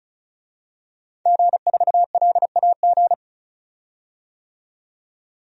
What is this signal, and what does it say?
Morse code, heard as a keyed tone, about 34 wpm, 700 Hz. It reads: G4LAG
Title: DXCC-100 Call Signs - 5 Characters - 35wpm